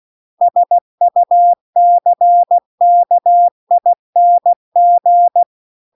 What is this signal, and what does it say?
Morse code, heard as a keyed tone, about 16 wpm, 700 Hz. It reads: SUCKING